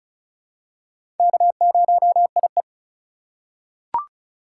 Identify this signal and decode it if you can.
Morse code, heard as a keyed tone, about 36 wpm, 700 Hz. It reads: K0IE
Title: USA Call Signs - 4 Characters - 2x Character Spacing - 35wpm